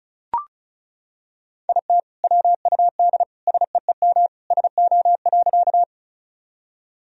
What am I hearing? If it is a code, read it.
Morse code, heard as a keyed tone, about 36 wpm, 700 Hz. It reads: IT WUD SEEM SO.
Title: Sentences from Top 300 Words and Abbreviations 35wpm